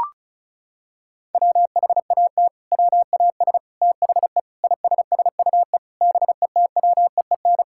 Morse code, heard as a keyed tone, about 36 wpm, 700 Hz. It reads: WHAT WAS THE ISSUE BETWEEN
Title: Sentences from Top 1000 Words - Review Part 6 - 35wpm